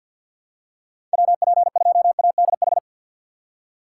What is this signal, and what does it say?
Morse code, heard as a keyed tone, about 46 wpm, 700 Hz. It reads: WP2ADH
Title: USA Call Signs - 3x Character Spacing - 50wpm